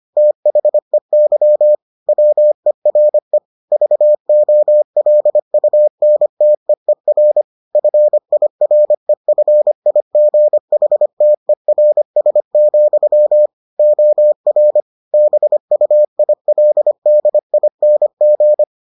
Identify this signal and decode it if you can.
Morse code, heard as a keyed tone, about 25 wpm, 600 Hz. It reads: THEY WERE VOLUNTEER FIREFIGHTERS, OR BUILDING